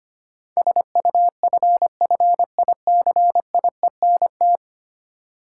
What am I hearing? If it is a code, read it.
Morse code, heard as a keyed tone, about 25 wpm, 700 Hz. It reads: SUFFICIENT
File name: 10 Letter Words - First 3 Letters Given - 25wpm